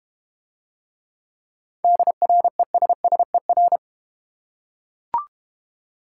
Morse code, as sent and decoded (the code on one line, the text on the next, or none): -.. .-. ... ... .-.
DRESSER